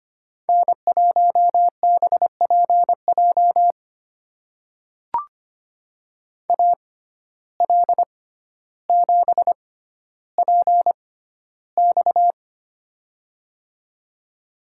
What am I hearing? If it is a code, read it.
Morse code, heard as a keyed tone, about 25 wpm, spaced as for 13 wpm, 700 Hz. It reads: N1BPJ AL7PX
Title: USA Call Signs - 5 Characters - 6x Character Spacing - 25wpm